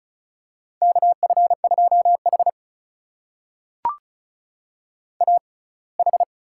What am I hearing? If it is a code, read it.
Morse code, heard as a keyed tone, about 36 wpm, spaced as for 24 wpm, 700 Hz. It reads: KF2H AH
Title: USA Call Signs - 4 Characters - 6x Character Spacing - 35wpm